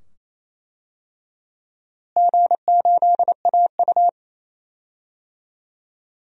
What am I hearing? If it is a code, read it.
Morse code, heard as a keyed tone, about 27 wpm, 700 Hz. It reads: G8AU
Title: DXCC-100 Call Signs - 4 Characters - Repeated 2x - 28wpm